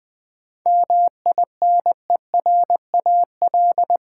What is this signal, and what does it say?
Morse code, heard as a keyed tone, about 20 wpm, 700 Hz. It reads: MINERAL
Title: 7 Letter Words - First 3 Letters Given - 20wpm